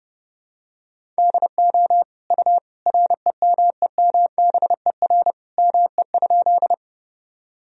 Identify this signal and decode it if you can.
Morse code, heard as a keyed tone, about 32 wpm, 700 Hz. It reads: DO U REMEMBER ME?
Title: Sentences from Top 400 Words and Abbreviations 30wpm